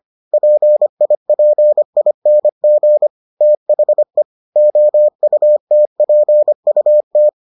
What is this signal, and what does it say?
Morse code, heard as a keyed tone, about 25 wpm, 600 Hz. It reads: PIPING THE OUTPUT